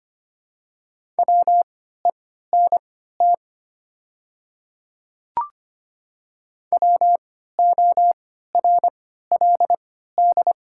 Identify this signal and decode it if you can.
Morse code, heard as a keyed tone, about 25 wpm, spaced as for 14 wpm, 700 Hz. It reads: WENT WORLD
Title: 3x Character Spacing - Top 200 Words - 25wpm